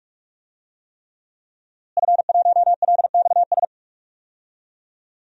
Morse code, heard as a keyed tone, about 42 wpm, 700 Hz. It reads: F1LXS
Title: DXCC-100 Call Signs - 5 Characters - Repeated 3x - 45wpm